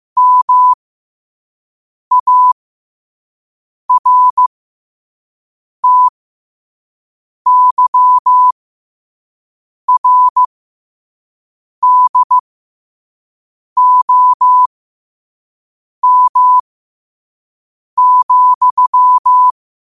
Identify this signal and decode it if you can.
Morse code, heard as a keyed tone, about 15 wpm, spaced as for 5 wpm, 1000 Hz. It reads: MARTYRDOM,
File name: qotdfiles / 5wpm